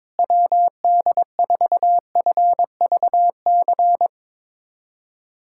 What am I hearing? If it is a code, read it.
Morse code, heard as a keyed tone, about 21 wpm, 700 Hz. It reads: WD4FVC
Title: USA Callsign DE Callsign 22wpm